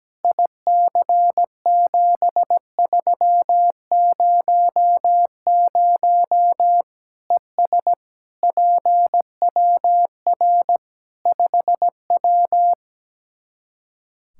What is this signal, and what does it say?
Morse code, heard as a keyed tone, about 17 wpm, 700 Hz. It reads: IC7300 ES PWR 5W
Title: Practice QSO - Exchange 2 - Info - 17wpm